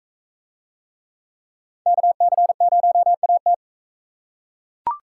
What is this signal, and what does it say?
Morse code, heard as a keyed tone, about 41 wpm, 700 Hz. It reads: KC0AT